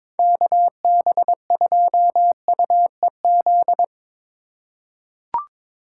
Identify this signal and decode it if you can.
Morse code, heard as a keyed tone, about 21 wpm, 700 Hz. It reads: KB2UEZ
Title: USA Call Signs - 6 Characters - 3x Character Spacing - 22wpm